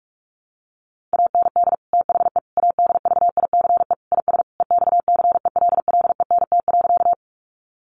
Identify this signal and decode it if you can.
Morse code, heard as a keyed tone, about 44 wpm, 700 Hz. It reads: AND THE ADVICE IS EXCELLENT.
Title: Sentences from Top 1100 Words 45wpm